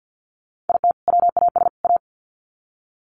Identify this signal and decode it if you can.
Morse code, heard as a keyed tone, about 50 wpm, 700 Hz. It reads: IT WAS A
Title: Sets of 3 Words 50wpm